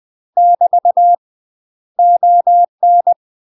Morse code, heard as a keyed tone, about 20 wpm, 700 Hz. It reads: <BT> ON
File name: UK MASTER QSOs TRAINING / 20WPM / UK MASTER QSOs 0012